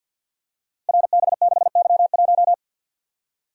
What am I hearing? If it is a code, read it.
Morse code, heard as a keyed tone, about 52 wpm, 700 Hz. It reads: RB6Y1
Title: Code Groups - 5 Letter:Numbers - 50wpm